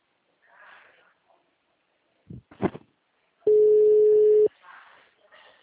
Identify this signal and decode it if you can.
Ringback tone (Europe, 425 Hz).